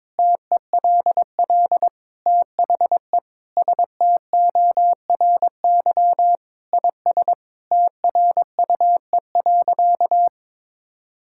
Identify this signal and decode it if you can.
Morse code, heard as a keyed tone, about 21 wpm, 700 Hz. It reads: TELL THE STORY IS TRUE.
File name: Sentences with 7-Word Phrases - 22wpm